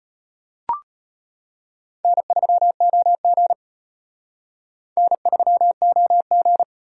Morse code, heard as a keyed tone, about 31 wpm, 700 Hz. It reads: N3OG N3OG
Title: Speed Racing - USA 1x2 Call Signs 25wpm